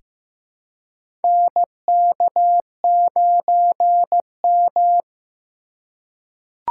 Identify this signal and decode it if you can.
Morse code, heard as a keyed tone, about 15 wpm, 700 Hz. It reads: NK9M
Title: USA Call Signs - 4 Characters - 2x Character Spacing - 15wpm